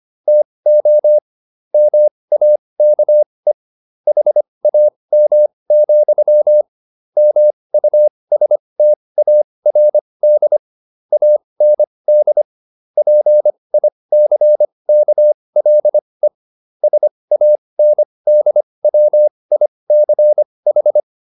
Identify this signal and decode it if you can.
Morse code, heard as a keyed tone, about 25 wpm, spaced as for 20 wpm, 600 Hz. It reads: TO MAKE HAM, MUSTARD AND PICKLE SANDWICH